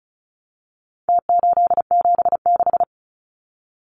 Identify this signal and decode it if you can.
Morse code, heard as a keyed tone, about 34 wpm, 700 Hz. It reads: T876
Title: Numbers - Contest Style - 4-Digit 35wpm